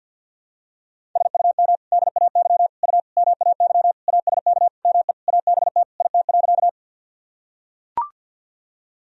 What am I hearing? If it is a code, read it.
Morse code, heard as a keyed tone, about 45 wpm, 700 Hz. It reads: SUM DAY U MAY ASK ME ABT IT.
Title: Sentences from Top 200 Words and Abbreviations 50wpm